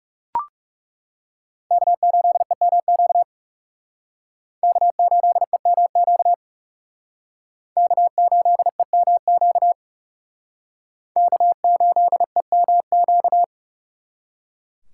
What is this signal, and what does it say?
Morse code, heard as a keyed tone, about 37 wpm, 700 Hz. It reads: K8EMQ K8EMQ K8EMQ K8EMQ
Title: Speed Racing - USA 1x3 Call Signs 30wpm